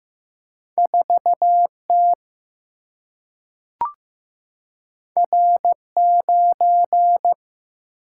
Morse code, heard as a keyed tone, about 15 wpm, 700 Hz. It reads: 4 T R 9
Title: Single-Letter Single-Number 15wpm